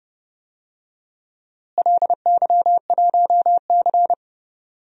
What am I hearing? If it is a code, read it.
Morse code, heard as a keyed tone, about 29 wpm, 700 Hz. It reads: LY1C
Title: DXCC-100 Call Signs - 4 Characters - Repeated 3x - 30wpm